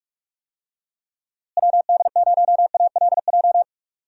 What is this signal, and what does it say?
Morse code, heard as a keyed tone, about 46 wpm, 700 Hz. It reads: WD0ALJ
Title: USA Call Signs - 6 Characters - Repeated 3x - 45wpm